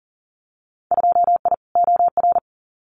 Morse code, heard as a keyed tone, about 40 wpm, 700 Hz. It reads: 2I OR
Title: Winter Field Day Exchanges Encore 40wpm